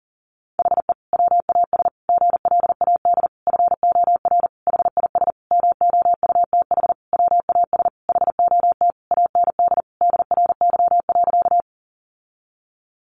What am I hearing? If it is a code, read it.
Morse code, heard as a keyed tone, about 40 wpm, 700 Hz. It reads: HE WAS GLAD FOR HIS MOUTH WAS HOT AND DRY.
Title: Sentences from Top 1500 Words 40wpm